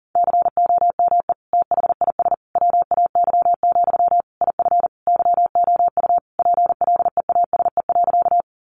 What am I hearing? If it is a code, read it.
Morse code, heard as a keyed tone, about 40 wpm, 700 Hz. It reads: COME THIS WAY, IF YOU PLEASE.
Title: Sentences from Top 1500 Words 40wpm